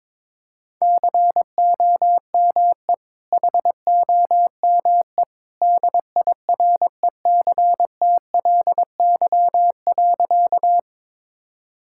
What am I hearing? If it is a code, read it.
Morse code, heard as a keyed tone, about 22 wpm, 700 Hz. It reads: COME HOME DIRECTLY.
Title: Sentences from Top 1000 Words - Review Part 5 - 22wpm